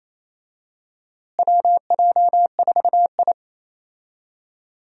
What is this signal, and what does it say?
Morse code, heard as a keyed tone, about 28 wpm, 700 Hz. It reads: WJ4I
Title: USA Call Signs - 4 Characters - Repeated 3x - 28wpm